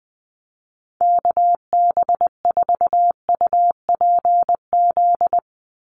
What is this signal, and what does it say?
Morse code, heard as a keyed tone, about 20 wpm, 700 Hz. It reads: KB4UPZ
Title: USA Call Signs 20wpm